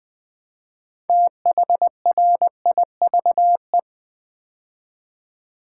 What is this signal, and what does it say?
Morse code, heard as a keyed tone, about 20 wpm, 700 Hz. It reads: THRIVE
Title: Build A Word - 3 Letters At A Time - 6-Letter Words - 20wpm